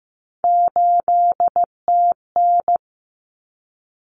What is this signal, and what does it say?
Morse code, heard as a keyed tone, about 15 wpm, 700 Hz. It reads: 8TN
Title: Common Cut Numbers - 3-Digit 15wpm